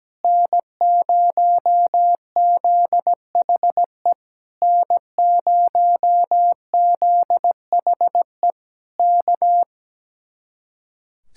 Morse code, heard as a keyed tone, about 17 wpm, 700 Hz. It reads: N0ZHE N0ZHE K
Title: Speed-Racing Common QSO Phrases - 17wpm